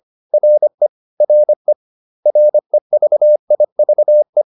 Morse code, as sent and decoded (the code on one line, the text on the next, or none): .-. / .-. / .-. ...- .. ...- .
RE RE REVIVE